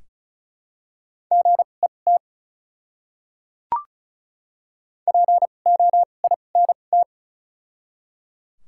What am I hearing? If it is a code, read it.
Morse code, heard as a keyed tone, about 34 wpm, spaced as for 25 wpm, 700 Hz. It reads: GET POINT
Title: 2x Character Spacing - Top 200 Words - 35wpm